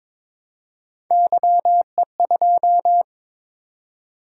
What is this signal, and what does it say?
Morse code, heard as a keyed tone, about 22 wpm, 700 Hz. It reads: YE2
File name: practice / 22wpm / Three Letter-Number Combinations - 22wpm